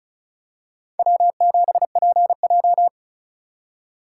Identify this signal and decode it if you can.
Morse code, heard as a keyed tone, about 36 wpm, 700 Hz. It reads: W7PJ
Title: USA Call Signs - 4 Characters - 2x Character Spacing - 35wpm